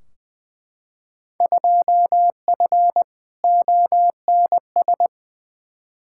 Morse code, heard as a keyed tone, about 20 wpm, 700 Hz. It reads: 2F ONS
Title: Field Day Exchanges - 20wpm